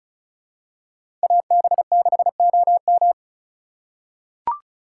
Morse code, heard as a keyed tone, about 34 wpm, 700 Hz. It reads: AB6OM
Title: USA Call Signs - 5 Characters - 2x Character Spacing - 35wpm